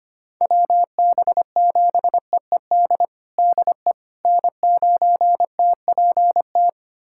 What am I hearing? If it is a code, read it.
Morse code, heard as a keyed tone, about 25 wpm, 700 Hz. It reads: WB7EED DE N9TPT
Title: USA Callsign DE Callsign 25wpm